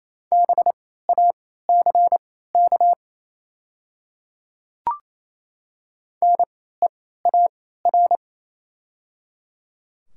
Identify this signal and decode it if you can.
Morse code, heard as a keyed tone, about 29 wpm, spaced as for 16 wpm, 690 Hz. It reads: BACK NEAR
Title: 3x Character Spacing - Top 200 Words - 28wpm